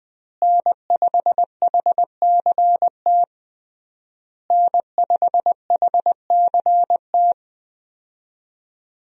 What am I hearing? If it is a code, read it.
Morse code, heard as a keyed tone, about 20 wpm, 700 Hz. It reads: N5HCT N5HCT
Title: USA Call Signs - 5 Characters - Repeated 3x - 20wpm